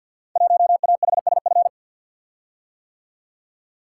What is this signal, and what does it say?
Morse code, heard as a keyed tone, about 50 wpm, 700 Hz. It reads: 1AHSF
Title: Code Groups - 5 Letter:Numbers - 50wpm